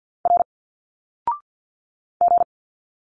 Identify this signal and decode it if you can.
Morse code, heard as a keyed tone, about 50 wpm, 700 Hz. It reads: R G
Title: Single Letters 50wpm